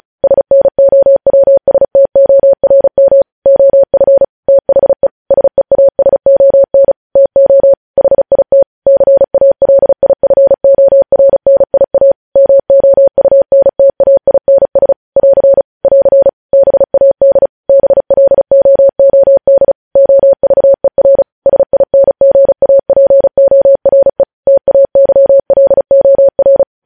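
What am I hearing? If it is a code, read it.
Morse code, heard as a keyed tone, about 36 wpm, 550 Hz. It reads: SNOWSTORM OF THE SEASON TO HIT CALIFORNIA MOUNTAINS <AR> <AR> BAD BLOOD OVER SINGAPORE TAYLOR
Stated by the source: Morse code news headlines sent at 35.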